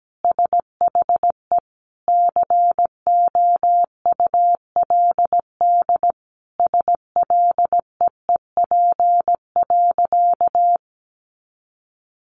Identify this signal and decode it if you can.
Morse code, heard as a keyed tone, about 17 wpm, 700 Hz. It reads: SHE COULD SLEEP.
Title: Sentences from Top 500 Words 17wpm